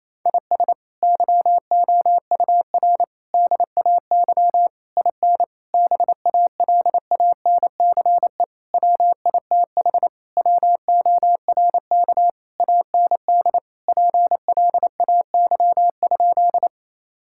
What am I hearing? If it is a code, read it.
Morse code, heard as a keyed tone, about 28 wpm, 700 Hz. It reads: IS YOUR DAY IN BALANCE WITH WORK AND PLAY?
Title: Sentences from Top 1000 Words - Review Part 6 - 28wpm